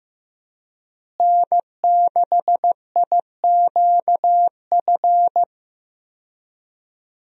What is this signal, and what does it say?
Morse code, heard as a keyed tone, about 15 wpm, 700 Hz. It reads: N6IQF
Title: USA Call Signs - 5 Characters - Repeated 3x - 15wpm